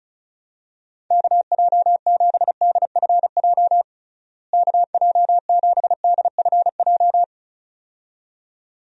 Morse code, heard as a keyed tone, about 35 wpm, 700 Hz. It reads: KJ7DFJ KJ7DFJ
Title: USA Call Signs - 6 Characters - Repeated 2x - 35wpm